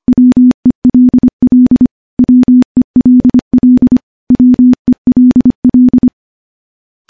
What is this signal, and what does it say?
Morse code, heard as a keyed tone, about 25 wpm, 250 Hz. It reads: WELL WELL WELL
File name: morse-training-25WPM